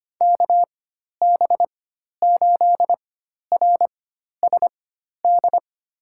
Morse code, heard as a keyed tone, about 25 wpm, spaced as for 12 wpm, 700 Hz. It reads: KB8RSD
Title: USA Call Signs - 6 Characters - 4x Character Spacing - 25wpm